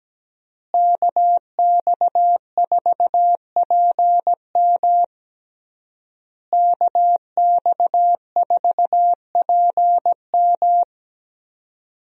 Morse code, heard as a keyed tone, about 17 wpm, 700 Hz. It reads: KX4PM KX4PM
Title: USA Call Signs - 5 Characters - Repeated 3x - 17wpm